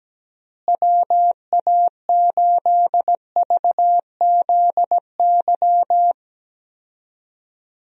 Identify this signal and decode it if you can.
Morse code, heard as a keyed tone, about 17 wpm, 700 Hz. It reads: WA8VZY
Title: USA Call Signs - 6 Characters - Repeated 3x - 17wpm